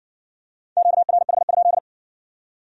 Morse code, heard as a keyed tone, about 55 wpm, 700 Hz. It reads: QRH?